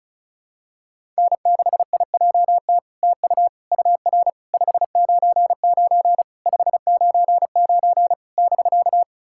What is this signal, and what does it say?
Morse code, heard as a keyed tone, about 36 wpm, 700 Hz. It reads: N6IJT TU UR 599 599 <BK>
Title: SOTA QSO Practice 35wpm